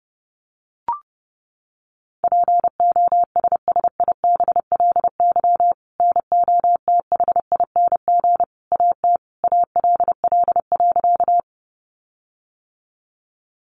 Morse code, heard as a keyed tone, about 30 wpm, 700 Hz. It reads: POSSIBLY NOTHING AT ALL.
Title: Sentences from Top 1300 Words 30wpm